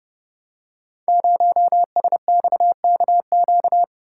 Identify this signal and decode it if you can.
Morse code, heard as a keyed tone, about 30 wpm, 700 Hz. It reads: 0SXKQ
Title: Code Groups - 5 Letter:Numbers - 30wpm